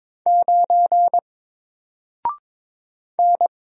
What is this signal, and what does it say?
Morse code, heard as a keyed tone, about 22 wpm, 700 Hz. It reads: 9 N
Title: Single Character (Letter:Number:Punctuation:Prosign) 22wpm